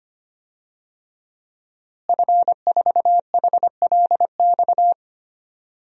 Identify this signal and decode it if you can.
Morse code, heard as a keyed tone, about 25 wpm, 700 Hz. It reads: F4HLX
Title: DXCC-100 Call Signs - 5 Characters - Repeated 2x - 25wpm